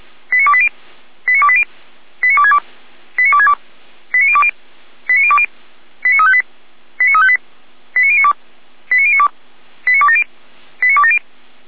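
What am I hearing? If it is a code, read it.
Five-tone selective call (ZVEI), heard as quick runs of five tones, tones of about 70 ms. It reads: 89280 89280 89262 89262 89020 89020 89358 89358 89092 89092 89270 89270
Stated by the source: ZVEI five-tone selective call received by SDR, archived by Feuerwehr Orlamünde (fire brigade).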